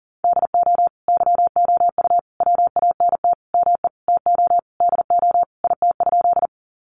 Morse code, heard as a keyed tone, about 40 wpm, 700 Hz. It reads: DO YOU WANT ME TO DO IT?
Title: Sentences from Top 200 Words 40wpm